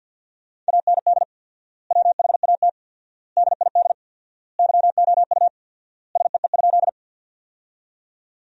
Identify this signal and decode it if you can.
Morse code, heard as a keyed tone, about 52 wpm, 700 Hz. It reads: AND WHAT DID YOU SEE?
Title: 4x Word-Spacing - Sentences from Top 100 Words - 50wpm